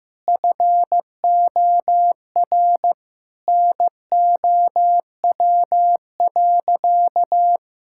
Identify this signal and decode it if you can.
Morse code, heard as a keyed tone, about 16 wpm, 700 Hz. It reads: FOR NOW.
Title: Sentences from Top 1000 Words - Review Part 6 - 15wpm